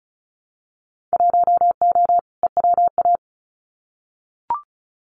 Morse code, heard as a keyed tone, about 36 wpm, 700 Hz. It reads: 1O EWA
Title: Winter Field Day Exchanges Encore 35wpm